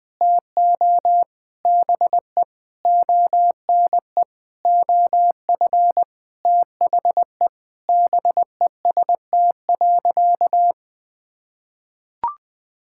Morse code, heard as a keyed tone, about 20 wpm, 700 Hz. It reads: TO BE ONE OF THE BEST.
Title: Sentences with 7-Word Phrases - 20wpm